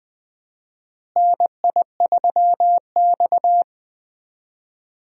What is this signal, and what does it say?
Morse code, heard as a keyed tone, about 20 wpm, 700 Hz. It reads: NI3X
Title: USA Call Signs - 4 Characters - Repeated 2x - 20wpm